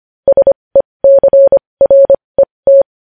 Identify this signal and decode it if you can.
Morse code, heard as a keyed tone, about 25 wpm, spaced as for 20 wpm, 550 Hz. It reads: SECRET
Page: News headlines in morse code at 20 wpm